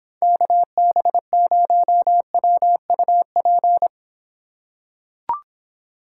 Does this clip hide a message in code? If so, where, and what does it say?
Yes: Morse code from 0.2 to 3.9 s, heard as a keyed tone, about 26 wpm, 700 Hz. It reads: KB0WUP